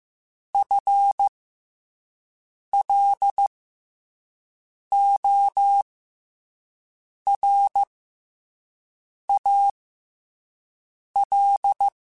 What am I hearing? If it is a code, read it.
Morse code, heard as a keyed tone, about 15 wpm, spaced as for 5 wpm, 800 Hz. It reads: FLORAL